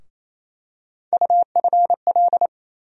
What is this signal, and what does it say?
Morse code, heard as a keyed tone, about 28 wpm, 700 Hz. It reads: UFL
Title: USA Call Sign Postfixes - 3 Letters 28wpm